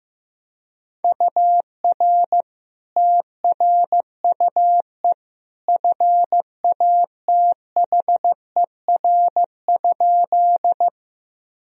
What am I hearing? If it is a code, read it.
Morse code, heard as a keyed tone, about 15 wpm, 700 Hz. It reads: UR TRUE FATHER?
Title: Sentences from Top 400 Words and Abbreviations 15wpm